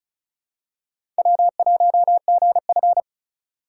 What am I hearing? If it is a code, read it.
Morse code, heard as a keyed tone, about 34 wpm, 700 Hz. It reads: W1GF
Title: USA Call Signs - 4 Characters - 2x Character Spacing - 35wpm